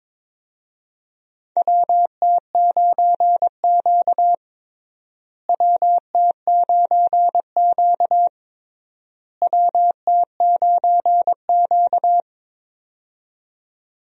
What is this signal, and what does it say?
Morse code, heard as a keyed tone, about 22 wpm, 700 Hz. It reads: WT9Q WT9Q WT9Q
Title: USA Call Signs - 4 Characters - Repeated 3x - 22wpm